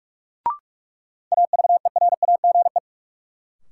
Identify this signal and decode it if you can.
Morse code, heard as a keyed tone, about 46 wpm, 700 Hz. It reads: AVERAGE
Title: Sentences from Top 1400 Words - New Word Drill Alt - 45wpm